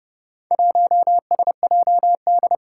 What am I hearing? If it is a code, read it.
Morse code, heard as a keyed tone, about 30 wpm, 700 Hz. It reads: 1SJD
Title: Four Challenging Letter-Number Combinations 30wpm